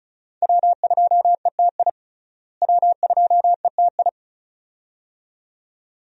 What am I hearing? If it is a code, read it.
Morse code, heard as a keyed tone, about 36 wpm, 700 Hz. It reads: W2ETI W2ETI
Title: USA Call Signs - 5 Characters - Repeated 2x - 35wpm